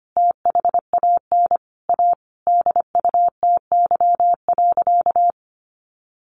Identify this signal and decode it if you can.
Morse code, heard as a keyed tone, about 25 wpm, 700 Hz. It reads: THAN A DUTY.
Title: Sentences from Top 1500 Words 25wpm